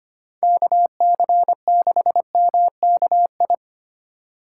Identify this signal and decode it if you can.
Morse code, heard as a keyed tone, about 25 wpm, 700 Hz. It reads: KC6MKI
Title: USA Call Signs - 6 Characters - 2x Character Spacing - 25wpm